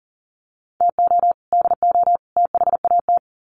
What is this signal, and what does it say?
Morse code, heard as a keyed tone, about 40 wpm, 700 Hz. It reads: TO DO THAT
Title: Sets of 3 Words 40wpm